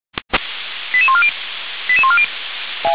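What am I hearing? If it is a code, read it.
Five-tone selective call (ZVEI), heard as quick runs of five tones, tones of about 70 ms. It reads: 88140 88140
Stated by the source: ZVEI five-tone selective call received on an SDR.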